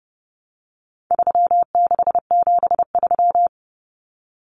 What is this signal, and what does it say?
Morse code, heard as a keyed tone, about 30 wpm, 700 Hz. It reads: 3673
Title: Common Cut Numbers - 4-Digit 30wpm